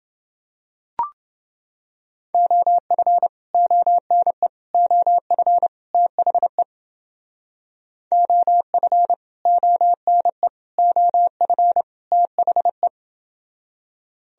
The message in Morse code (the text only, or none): OF ONE OF THE OF ONE OF THE